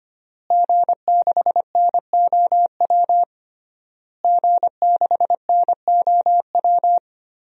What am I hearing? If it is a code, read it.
Morse code, heard as a keyed tone, about 25 wpm, 700 Hz. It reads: G6NOW G6NOW
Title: DXCC-100 Call Signs - 5 Characters - Repeated 3x - 25wpm